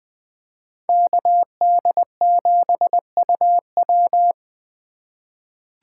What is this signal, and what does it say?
Morse code, heard as a keyed tone, about 20 wpm, 700 Hz. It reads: KD7UW
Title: USA Call Signs - 5 Characters - 4x Character Spacing - 20wpm